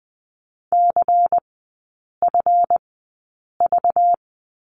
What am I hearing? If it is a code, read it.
Morse code, heard as a keyed tone, about 20 wpm, spaced as for 8 wpm, 700 Hz. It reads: CFV
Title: 191 Lesson 31e - Letter v - Three Characters 20wpm